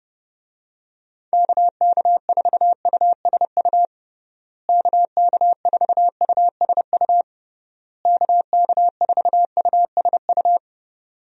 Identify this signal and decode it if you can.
Morse code, heard as a keyed tone, about 32 wpm, 700 Hz. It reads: KK4USU KK4USU KK4USU
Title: USA Call Signs - 6 Characters - Repeated 3x - 30wpm